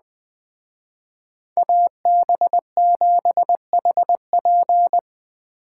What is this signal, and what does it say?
Morse code, heard as a keyed tone, about 20 wpm, 700 Hz. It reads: AB7HP
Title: Speed Racing - USA 2x2 Call Signs 17wpm